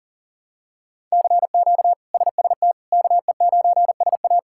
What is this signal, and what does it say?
Morse code, heard as a keyed tone, about 40 wpm, 700 Hz. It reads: CQ SST KE9SA
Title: K1USN SST QSO Practice - 40wpm